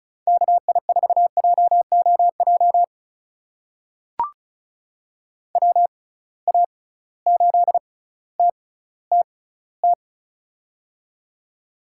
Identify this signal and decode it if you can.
Morse code, heard as a keyed tone, about 36 wpm, spaced as for 18 wpm, 700 Hz. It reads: KI4JOJ WA8TTT
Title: USA Call Signs - 6 Characters - 6x Character Spacing - 35wpm